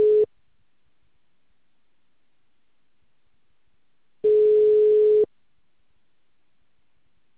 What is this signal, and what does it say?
Ringback tone (Europe, 425 Hz).